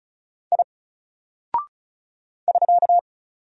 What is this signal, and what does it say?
Morse code, heard as a keyed tone, about 35 wpm, 700 Hz. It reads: I <SK>
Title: Single Character (Letter:Number:Punctuation:Prosign) 35wpm